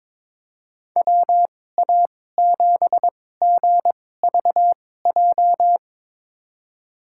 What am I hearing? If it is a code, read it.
Morse code, heard as a keyed tone, about 22 wpm, spaced as for 16 wpm, 700 Hz. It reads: WA7GVJ
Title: USA Call Signs - 2x Character Spacing - 22wpm